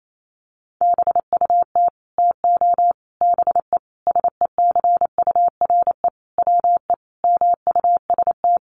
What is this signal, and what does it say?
Morse code, heard as a keyed tone, about 28 wpm, 700 Hz. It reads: BUT TO BE SECURE WE MUST
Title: Sentences from Top 1200 Words 28wpm